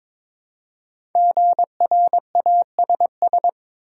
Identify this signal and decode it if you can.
Morse code, heard as a keyed tone, about 22 wpm, 700 Hz. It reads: GRASS
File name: 5 Letter Words - First 3 Letters Given - 22wpm